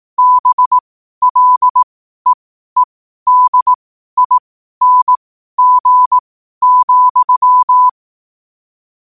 Morse code, heard as a keyed tone, about 18 wpm, spaced as for 12 wpm, 1000 Hz. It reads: BLEEDING,
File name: qotdfiles / 12wpm